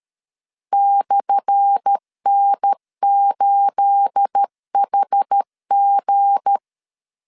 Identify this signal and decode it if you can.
Morse code, heard as a keyed tone, about 13 wpm, 800 Hz. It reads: /N8HG
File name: LEZIONI / 10 SIGNS / 10A SLASH 12wpm 2314